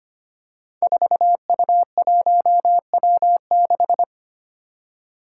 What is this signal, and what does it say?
Morse code, heard as a keyed tone, about 25 wpm, 700 Hz. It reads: 4U1W6
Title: Five Challenging Letter-Number Combinations 25wpm